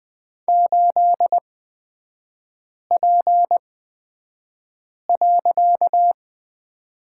Morse code, heard as a keyed tone, about 20 wpm, spaced as for 5 wpm, 700 Hz. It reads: 8P.
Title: 255 Lesson 41e - Period - Three Characters 20wpm